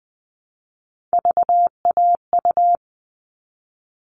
Morse code, heard as a keyed tone, about 20 wpm, 700 Hz. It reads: VAU
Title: All Cut Numbers - 3-Digit 20wpm